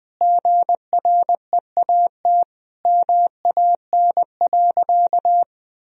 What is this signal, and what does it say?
Morse code, heard as a keyed tone, about 21 wpm, 700 Hz. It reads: GREAT MAN.
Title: Sentences from Top 1000 Words - Review Part 2 - 20wpm